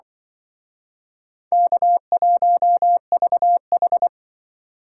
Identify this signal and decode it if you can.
Morse code, heard as a keyed tone, about 25 wpm, 700 Hz. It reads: K1VH
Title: Speed Racing - USA 1x2 Call Signs 20wpm